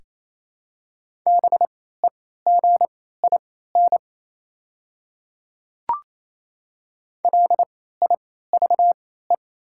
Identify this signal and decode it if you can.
Morse code, heard as a keyed tone, about 28 wpm, spaced as for 16 wpm, 700 Hz. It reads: BEGIN LIVE
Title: 3x Character Spacing - Top 300 Words - 28wpm